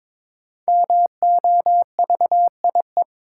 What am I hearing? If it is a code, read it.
Morse code, heard as a keyed tone, about 22 wpm, 700 Hz. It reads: MOVIE
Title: Sentences from Top 1000 Words - Review Part 5 - New Word Drill - 22wpm